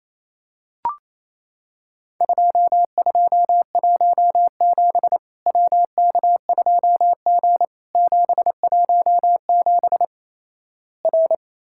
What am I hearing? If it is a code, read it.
Morse code, heard as a keyed tone, about 28 wpm, 700 Hz. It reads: 2217 WK2G 717 R
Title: Worked All Europe DX Contest - QTC Exchange - 28wpm